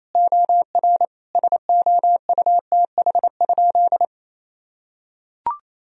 Morse code, heard as a keyed tone, about 28 wpm, 700 Hz. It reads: OR SOUTH?
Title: Sentences from Top 1000 Words - Review Part 2 - 28wpm